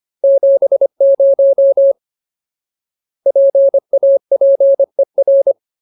Morse code, heard as a keyed tone, about 25 wpm, 550 Hz. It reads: 70 PAPER
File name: Ultimate Word List / 25WPM Ultimate